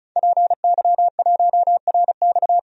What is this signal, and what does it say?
Morse code, heard as a keyed tone, about 35 wpm, 700 Hz. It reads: PY1RX
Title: DXCC-100 Call Signs - 5 Characters - 35wpm